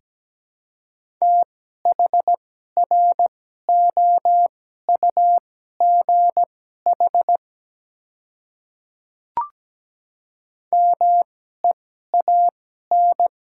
Morse code, heard as a keyed tone, about 17 wpm, spaced as for 12 wpm, 700 Hz. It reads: THROUGH MEAN
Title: 2x Character Spacing - Top 200 Words - 17wpm